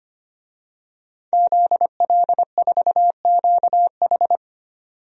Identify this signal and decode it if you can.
Morse code, heard as a keyed tone, about 25 wpm, 700 Hz. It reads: ZL4QH
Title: DXCC-100 Call Signs - 5 Characters - 25wpm